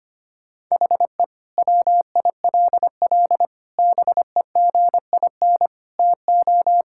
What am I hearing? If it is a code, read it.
Morse code, heard as a keyed tone, about 25 wpm, 700 Hz. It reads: HE WILL BEGIN TO
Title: Sentences from Top 1600 Words 25wpm